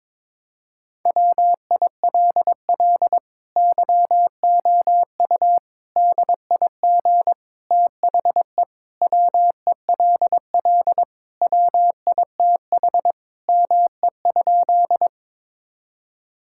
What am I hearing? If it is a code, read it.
Morse code, heard as a keyed tone, about 22 wpm, 700 Hz. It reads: WILL YOU DIG THE WELL WITH ME?
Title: Sentences from Top 1600 Words 22wpm